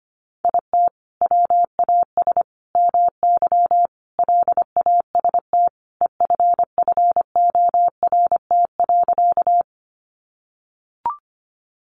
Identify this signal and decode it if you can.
Morse code, heard as a keyed tone, about 25 wpm, 700 Hz. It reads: IT WAS MY LAST EFFORT.
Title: Sentences from Top 1200 Words 25wpm